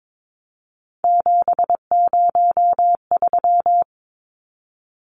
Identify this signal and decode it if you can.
Morse code, heard as a keyed tone, about 22 wpm, 700 Hz. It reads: 703
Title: Numbers - 3-Digit 22wpm